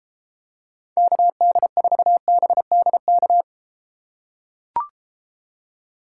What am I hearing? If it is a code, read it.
Morse code, heard as a keyed tone, about 34 wpm, 700 Hz. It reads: KD4BDK